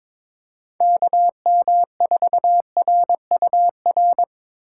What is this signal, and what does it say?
Morse code, heard as a keyed tone, about 22 wpm, 700 Hz. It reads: KM4RUR
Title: USA Call Signs - 6 Characters - 8x Character Spacing - 22wpm